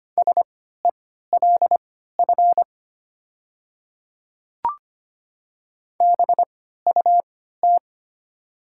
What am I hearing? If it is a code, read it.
Morse code, heard as a keyed tone, about 25 wpm, spaced as for 14 wpm, 700 Hz. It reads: SELF BUT
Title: 3x Character Spacing - Top 300 Words - 25wpm